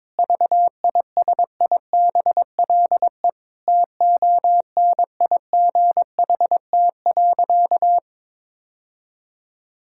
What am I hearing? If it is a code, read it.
Morse code, heard as a keyed tone, about 22 wpm, 700 Hz. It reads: VISIBLE TONIGHT.
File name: practice / 22wpm / Sentences with 2-Word Phrases - 22wpm